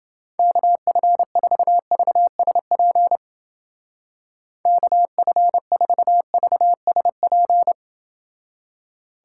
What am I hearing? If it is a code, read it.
Morse code, heard as a keyed tone, about 29 wpm, 700 Hz. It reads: KF4VSP KF4VSP